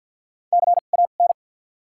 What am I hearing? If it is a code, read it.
Morse code, heard as a keyed tone, about 48 wpm, 700 Hz. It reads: C A N
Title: 2x Character Spacing - Top 200 Words - 50wpm